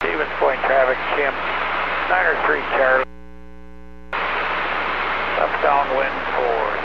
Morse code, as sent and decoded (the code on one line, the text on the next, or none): none